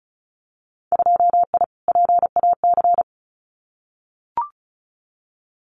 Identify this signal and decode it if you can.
Morse code, heard as a keyed tone, about 36 wpm, 700 Hz. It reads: 2I PAC
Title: Winter Field Day Exchanges Encore 35wpm